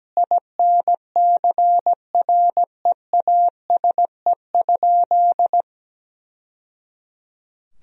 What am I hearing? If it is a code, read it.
Morse code, heard as a keyed tone, about 17 wpm, 700 Hz. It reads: INCREASE?
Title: Sentences from Top 1000 Words - Review Part 4 - 17wpm